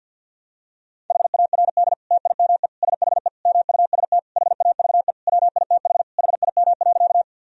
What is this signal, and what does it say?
Morse code, heard as a keyed tone, about 51 wpm, 700 Hz. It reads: HARD TIME SHE MUST HAVE WITH HIM.
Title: Sentences from Top 1000 Words - Review Part 1 - 50wpm